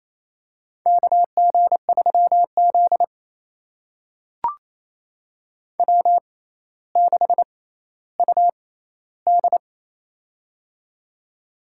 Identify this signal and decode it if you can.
Morse code, heard as a keyed tone, about 28 wpm, spaced as for 14 wpm, 700 Hz. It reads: KG3Z W6UD
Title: USA Call Signs - 6x Character Spacing - 28wpm